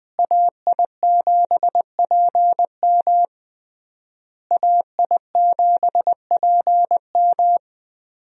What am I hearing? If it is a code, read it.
Morse code, heard as a keyed tone, about 20 wpm, 700 Hz. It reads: AI7PM AI7PM
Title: USA Call Signs - 5 Characters - Repeated 3x - 20wpm